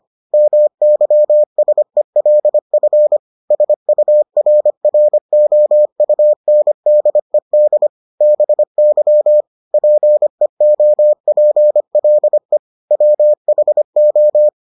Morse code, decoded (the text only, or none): MYSELF SURROUNDED BY PEOPLE WHO